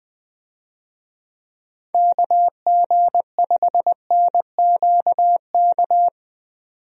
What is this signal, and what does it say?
Morse code, heard as a keyed tone, about 20 wpm, 700 Hz. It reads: KG5NQK
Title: USA Call Signs - 6 Characters - 20wpm